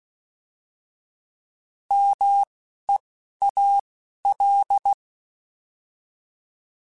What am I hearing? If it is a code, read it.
Morse code, heard as a keyed tone, about 16 wpm, 800 Hz. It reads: M E A L
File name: Alphabetical / Alphabetical mp3s / Ms @10